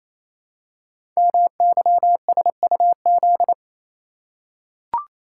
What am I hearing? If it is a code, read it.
Morse code, heard as a keyed tone, about 28 wpm, 700 Hz. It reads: MYSUZ
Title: Code Groups - 5 Letters - 28wpm